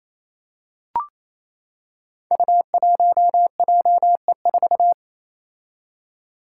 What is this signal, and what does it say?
Morse code, heard as a keyed tone, about 28 wpm, 700 Hz. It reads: U1JE4
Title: Five Challenging Letter-Number Combinations 28wpm